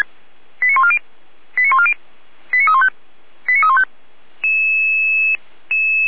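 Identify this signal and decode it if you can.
Five-tone selective call (ZVEI), heard as quick runs of five tones, tones of about 70 ms. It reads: 89140 89140 89315 89315
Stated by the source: ZVEI five-tone selective call received by SDR, from the feuerwehr-orlamuende.org (Orlamünde fire brigade) audio server.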